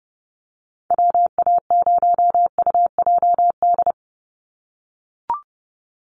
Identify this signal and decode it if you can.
Morse code, heard as a keyed tone, about 30 wpm, 700 Hz. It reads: WA0UJD
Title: USA Call Signs 30wpm